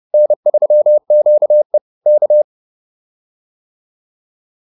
Morse code, heard as a keyed tone, about 30 wpm, 600 Hz. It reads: N3QE K
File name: QSO101 30